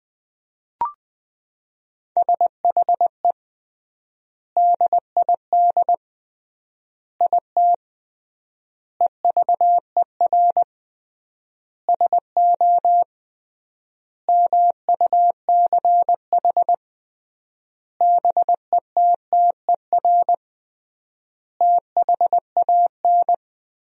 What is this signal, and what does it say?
Morse code, heard as a keyed tone, about 20 wpm, 700 Hz. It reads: SHE DID IT EVER SO MUCH BETTER THAN
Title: 3x Word-Spacing - Sentences from Top 400 Words - 20wpm